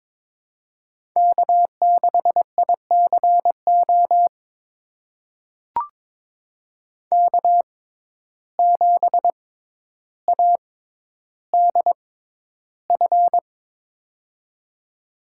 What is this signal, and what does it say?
Morse code, heard as a keyed tone, about 22 wpm, spaced as for 11 wpm, 700 Hz. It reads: K6ICO K7ADF
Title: USA Call Signs - 5 Characters - 6x Character Spacing - 22wpm